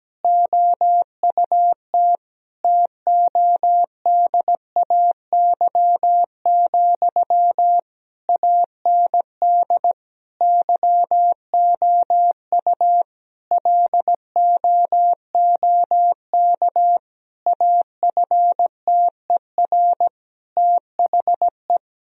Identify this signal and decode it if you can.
Morse code, heard as a keyed tone, about 17 wpm, 700 Hz. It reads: OUT TODAY, AND YOU LOOK AFTER THE
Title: Sentences from Top 1000 Words - Review Part 4 - 17wpm